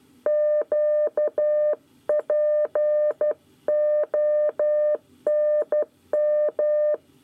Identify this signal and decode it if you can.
Morse code, heard as a keyed tone, about 12 wpm, 600 Hz. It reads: QPONM
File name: 9 lezione cw 1